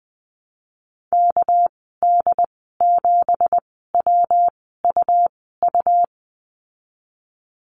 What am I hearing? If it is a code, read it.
Morse code, heard as a keyed tone, about 20 wpm, spaced as for 14 wpm, 700 Hz. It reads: KD7WUU
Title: 200 Lesson 32g - Letter k - Call Signs 20wpm